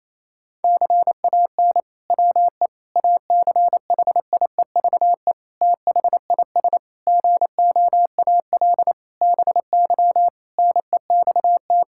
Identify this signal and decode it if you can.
Morse code, heard as a keyed tone, about 27 wpm, 700 Hz. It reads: CAN WE ACHIEVE THIS GOAL BY NEXT
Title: Sentences from Top 1000 Words - Review Part 5 - 28wpm